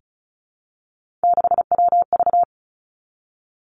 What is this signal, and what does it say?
Morse code, heard as a keyed tone, about 36 wpm, 700 Hz. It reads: 6WV
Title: Three Challenging Letter-Number Combinations 35wpm